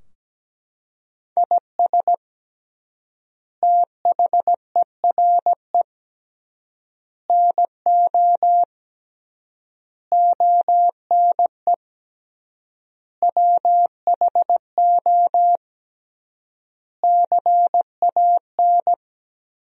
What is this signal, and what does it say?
Morse code, heard as a keyed tone, about 17 wpm, 700 Hz. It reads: IS THERE NO ONE WHO CAN
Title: 3x Word-Spacing - Sentences from Top 200 Words - 17wpm